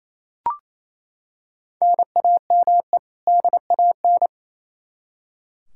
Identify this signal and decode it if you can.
Morse code, heard as a keyed tone, about 27 wpm, 700 Hz. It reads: NAME DAN
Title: Practice QSO - Exchange 1 - Beginning - 28wpm